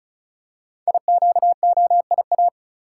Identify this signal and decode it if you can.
Morse code, heard as a keyed tone, about 36 wpm, 700 Hz. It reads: IQOIA
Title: Code Groups - 5 Letters - 35wpm